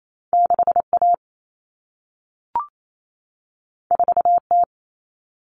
Morse code, heard as a keyed tone, about 28 wpm, 700 Hz. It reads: 6A 4T
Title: Common Cut Numbers - 2-Digit 28wpm